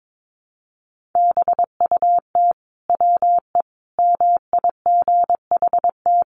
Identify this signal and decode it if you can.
Morse code, heard as a keyed tone, about 22 wpm, 700 Hz. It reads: BUT WE MIGHT
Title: Sentences from Top 1300 Words 22wpm